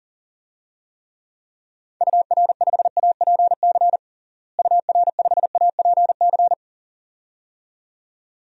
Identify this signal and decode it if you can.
Morse code, heard as a keyed tone, about 40 wpm, 700 Hz. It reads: UR5APC UR5APC
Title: DXCC-100 Call Signs - 6 Characters - Repeated 2x - 40wpm